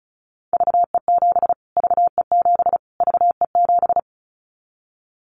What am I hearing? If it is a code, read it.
Morse code, heard as a keyed tone, about 34 wpm, 700 Hz. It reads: VE7 VE7 VE7
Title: Callsign Prefixes for Top 25 DXCC - Repeated 3x - 35wpm